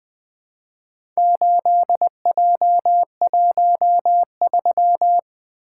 Morse code, heard as a keyed tone, about 20 wpm, 700 Hz. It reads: 8J13
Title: Four Letter-Number Combinations - 20wpm